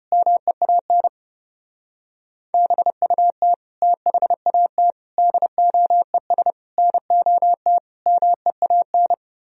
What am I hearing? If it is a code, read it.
Morse code, heard as a keyed tone, about 33 wpm, 700 Hz. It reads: MEAN BUT THAT DOES NOT MEAN